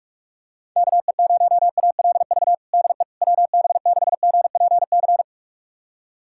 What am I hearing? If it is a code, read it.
Morse code, heard as a keyed tone, about 45 wpm, 700 Hz. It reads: KE0ALV DE WB6GPC
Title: USA Callsign DE Callsign - 6 Characters - 45wpm